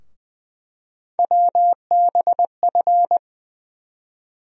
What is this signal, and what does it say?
Morse code, heard as a keyed tone, about 21 wpm, 700 Hz. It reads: WBF
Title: USA Call Sign Postfixes - 3 Letters 20wpm